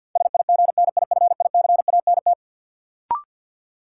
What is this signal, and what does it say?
Morse code, heard as a keyed tone, about 53 wpm, 700 Hz. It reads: SIGNIFICANT
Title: Sentences from Top 1000 Words - Review Part 6 - New Word Drill - 50wpm